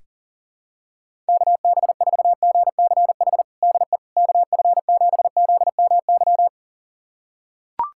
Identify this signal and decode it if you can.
Morse code, heard as a keyed tone, about 37 wpm, 700 Hz. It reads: KB4GCH DE KF7ZMY
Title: USA Callsign DE Callsign 40wpm